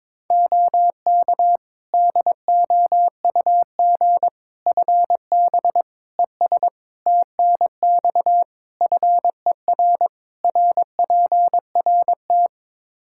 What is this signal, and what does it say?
Morse code, heard as a keyed tone, about 22 wpm, 700 Hz. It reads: OK DOUG FB ES TNX FER RPRT
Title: Practice QSO - Exchange 2 - Info - 22wpm